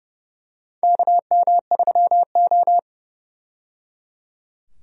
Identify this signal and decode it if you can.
Morse code, heard as a keyed tone, about 30 wpm, 700 Hz. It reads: KM3O
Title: USA Call Signs - 4 Characters - 30wpm